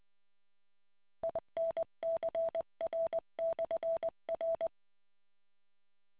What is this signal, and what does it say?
Morse code, heard as a keyed tone, about 20 wpm, 660 Hz. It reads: INCR/R